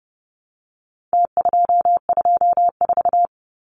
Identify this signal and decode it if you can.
Morse code, heard as a keyed tone, about 30 wpm, 700 Hz. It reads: T224
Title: Numbers - Contest Style - 4-Digit 30wpm